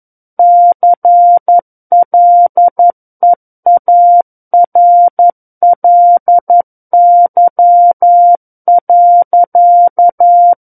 Morse code, heard as a keyed tone, about 11 wpm, 700 Hz. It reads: CLEARLY.
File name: 11 WPM